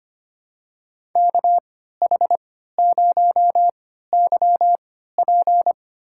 Morse code, heard as a keyed tone, about 25 wpm, spaced as for 14 wpm, 700 Hz. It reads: KH0YP
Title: USA Call Signs - 3x Character Spacing - 25wpm